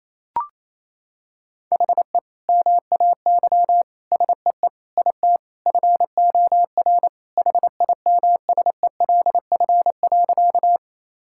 Morse code, heard as a keyed tone, about 28 wpm, 700 Hz. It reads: HE MAY SEE IT FOR HIMSELF.
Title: Sentences from Top 1600 Words 28wpm